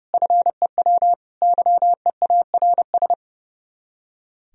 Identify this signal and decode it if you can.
Morse code, heard as a keyed tone, about 36 wpm, 700 Hz. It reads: FEW YEARS
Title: Speed-Racing - Sets of 3 Words 30wpm